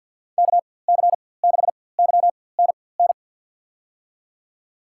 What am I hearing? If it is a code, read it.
Morse code, heard as a keyed tone, about 52 wpm, spaced as for 23 wpm, 700 Hz. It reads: KC6YNN
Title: USA Call Signs - 6 Characters - 4x Character Spacing - 50wpm